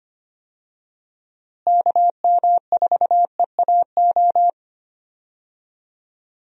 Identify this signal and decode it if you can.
Morse code, heard as a keyed tone, about 25 wpm, 700 Hz. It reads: KM4EAO
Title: USA Call Signs - 6 Characters - Repeated 3x - 25wpm